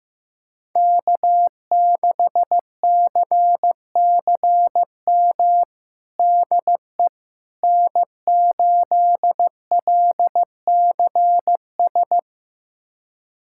Morse code, heard as a keyed tone, about 15 wpm, 700 Hz. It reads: K6CCM DE N8LCS
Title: USA Callsign DE Callsign - 5 Characters - 15wpm